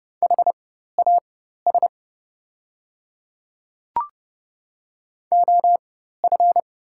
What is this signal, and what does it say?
Morse code, heard as a keyed tone, about 30 wpm, spaced as for 14 wpm, 700 Hz. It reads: HAS OF
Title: 4x Character Spacing - Top 300 Words - 30wpm